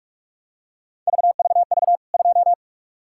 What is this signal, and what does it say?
Morse code, heard as a keyed tone, about 45 wpm, 700 Hz. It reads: VVV 2
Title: K1USN SST QSO Practice 45wpm